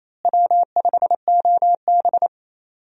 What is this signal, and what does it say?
Morse code, heard as a keyed tone, about 28 wpm, 700 Hz. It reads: W5OB
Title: USA Call Signs - 4 Characters - Repeated 2x - 28wpm